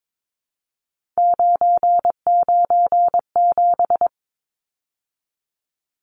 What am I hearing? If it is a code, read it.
Morse code, heard as a keyed tone, about 22 wpm, 700 Hz. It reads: 997
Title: Numbers - 3-Digit 22wpm